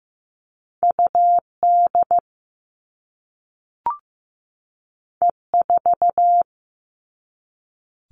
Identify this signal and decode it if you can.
Morse code, heard as a keyed tone, about 15 wpm, 700 Hz. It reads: UD E4
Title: All Cut Numbers - 2-Digit 15wpm